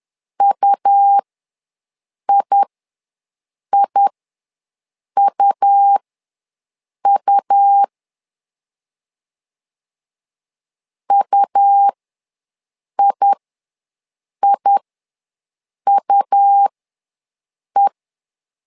Morse code, heard as a keyed tone, about 11 wpm, spaced as for 6 wpm, 800 Hz. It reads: UIIUU UIIUE
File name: LEZIONI / 2 UHDN / 2 U ws 10wpm 2017